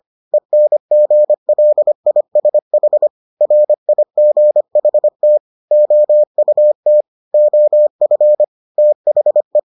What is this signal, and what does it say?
Morse code, heard as a keyed tone, about 25 wpm, 600 Hz. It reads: ENGLISH RIGHT OUT OF THE